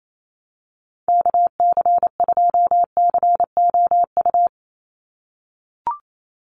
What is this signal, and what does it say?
Morse code, heard as a keyed tone, about 28 wpm, 700 Hz. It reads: KC2COU
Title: USA Call Signs 28wpm